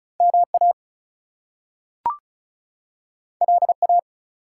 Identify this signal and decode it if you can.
Morse code, heard as a keyed tone, about 36 wpm, 700 Hz. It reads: MA LA
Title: US States and Canadian Provinces 35wpm